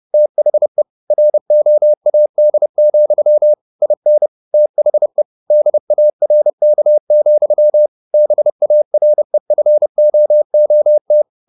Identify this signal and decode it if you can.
Morse code, heard as a keyed tone, about 30 wpm, 600 Hz. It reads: THE ROAD, IN THE DARK, BAREFOOT
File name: the old man and the sea / the old man and the sea 1 30wpm